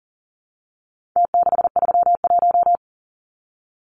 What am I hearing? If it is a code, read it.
Morse code, heard as a keyed tone, about 40 wpm, 700 Hz. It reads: T631
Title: Numbers - Contest Style - 4-Digit 40wpm